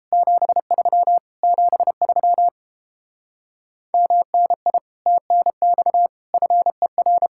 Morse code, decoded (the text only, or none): 73 73 MNI TNX FER